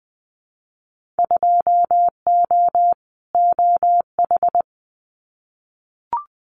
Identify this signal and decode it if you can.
Morse code, heard as a keyed tone, about 20 wpm, 700 Hz. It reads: 2O OH
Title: Winter Field Day Exchanges 20wpm